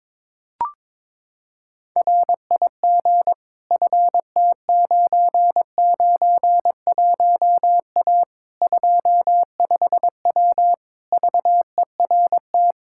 Morse code, heard as a keyed tone, about 22 wpm, 700 Hz. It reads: RIG FT991A 25W VERT
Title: Practice QSO - Exchange 2 - Info - 22wpm